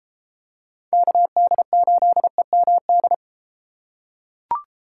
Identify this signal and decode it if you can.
Morse code, heard as a keyed tone, about 33 wpm, 700 Hz. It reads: KD8EMD